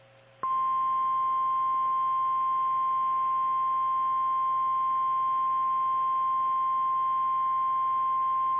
Emergency Alert System (SAME) tones (the 1050 Hz attention tone). It is heard from 0.4 to 7.6 s.
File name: eas / KKPRKXPN / KKPRKXPN